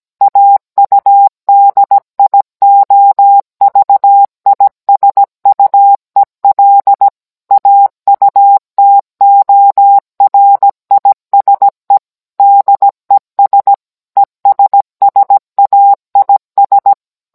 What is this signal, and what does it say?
Morse code, heard as a keyed tone, about 17 wpm, 800 Hz. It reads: AUDIOVISUEL AUTORISE DES ESSAIS